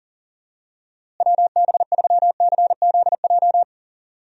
Morse code, heard as a keyed tone, about 37 wpm, 700 Hz. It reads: WB3CZJ
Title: USA Call Signs - 6 Characters - Repeated 2x - 40wpm